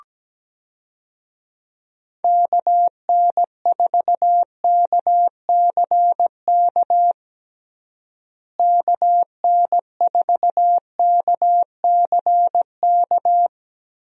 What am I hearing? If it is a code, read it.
Morse code, heard as a keyed tone, about 17 wpm, 700 Hz. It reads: KN4KCK KN4KCK
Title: USA Call Signs - 6 Characters - Repeated 2x - 17wpm